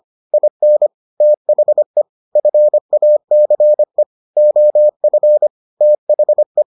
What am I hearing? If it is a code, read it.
Morse code, heard as a keyed tone, about 25 wpm, 600 Hz. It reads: IN THE FACE OF THE